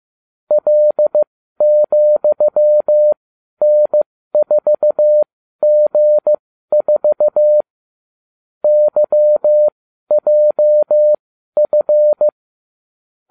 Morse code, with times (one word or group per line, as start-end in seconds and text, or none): 0.5-7.6 L,N4G4
8.6-12.3 YJF